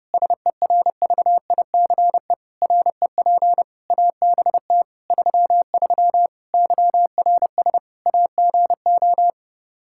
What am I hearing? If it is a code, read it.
Morse code, heard as a keyed tone, about 30 wpm, 700 Hz. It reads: SERVICE REP ABT 33 YRS AGO
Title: Practice QSO - Exchange 3 - More Info - 30wpm